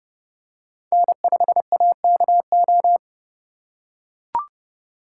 Morse code, heard as a keyed tone, about 29 wpm, 700 Hz. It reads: N5AKO